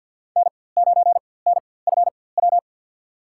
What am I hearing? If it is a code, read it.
Morse code, heard as a keyed tone, about 53 wpm, spaced as for 24 wpm, 700 Hz. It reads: N9NFW
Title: USA Call Signs - 5 Characters - 4x Character Spacing - 50wpm